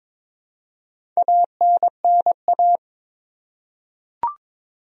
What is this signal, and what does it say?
Morse code, heard as a keyed tone, about 22 wpm, 700 Hz. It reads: ANNA
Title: USA Common Names - Max 5 Letters - 22wpm